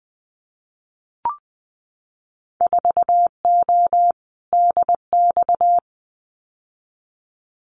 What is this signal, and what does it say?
Morse code, heard as a keyed tone, about 20 wpm, 700 Hz. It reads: 4O DX
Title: Winter Field Day Exchanges Encore 20wpm